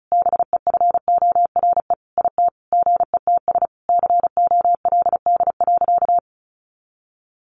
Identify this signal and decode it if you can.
Morse code, heard as a keyed tone, about 34 wpm, 700 Hz. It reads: BEFORE IT GETS COLD.
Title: Sentences from Top 1300 Words 35wpm